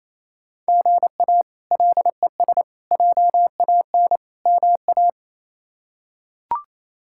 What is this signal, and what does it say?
Morse code, heard as a keyed tone, about 27 wpm, 700 Hz. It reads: GA LES JAN MA
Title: K1USN SST QSO Practice - 28wpm